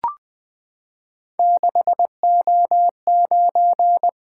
Morse code, heard as a keyed tone, about 20 wpm, 700 Hz. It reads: 6O9
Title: Three Letter-Number Combinations - 20wpm